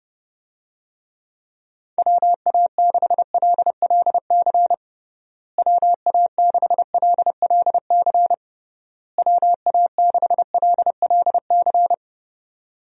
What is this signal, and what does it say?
Morse code, heard as a keyed tone, about 30 wpm, 700 Hz. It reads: WA6LLC WA6LLC WA6LLC
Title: USA Call Signs - 6 Characters - Repeated 3x - 30wpm